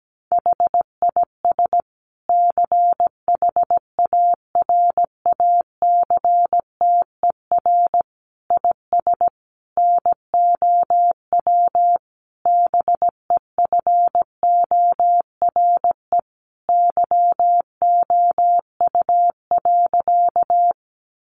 Morse code, heard as a keyed tone, about 17 wpm, 700 Hz. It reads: HIS CHARACTER IS NOW BEFORE YOU.
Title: Sentences from Top 1200 Words 17wpm